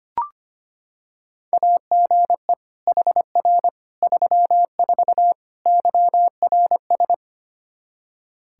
Morse code, heard as a keyed tone, about 25 wpm, 700 Hz. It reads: AGE HR 34 YRS
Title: Practice QSO - Exchange 3 - More Info - 25wpm